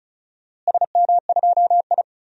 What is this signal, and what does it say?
Morse code, heard as a keyed tone, about 36 wpm, 700 Hz. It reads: SM2I
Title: DXCC-100 Call Signs - 4 Characters - Repeated 3x - 35wpm